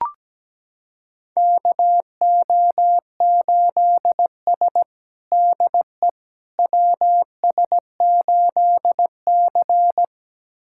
Morse code, heard as a keyed tone, about 17 wpm, 700 Hz. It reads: KO8S DE WS8C
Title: USA Callsign DE Callsign - 4 Characters - 17wpm